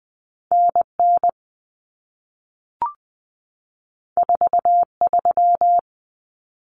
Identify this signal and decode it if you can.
Morse code, heard as a keyed tone, about 20 wpm, 700 Hz. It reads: NN 43
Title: Common Cut Numbers - 2-Digit 20wpm